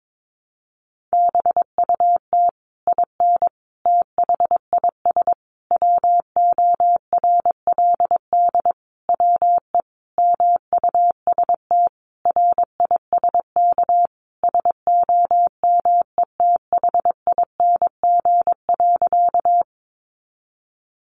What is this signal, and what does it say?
Morse code, heard as a keyed tone, about 22 wpm, 700 Hz. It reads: BUT IN THIS WORLD WE MUST RISK SOMETHING.
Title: Sentences from Top 600 Words 22wpm